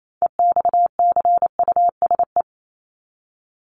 Morse code, heard as a keyed tone, about 28 wpm, 700 Hz. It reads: EXCUSE
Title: Top 501-750 Words 28wpm